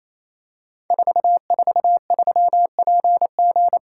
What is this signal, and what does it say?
Morse code, heard as a keyed tone, about 27 wpm, 700 Hz. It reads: 443PG
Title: Code Groups - 5 Letter:Numbers - 28wpm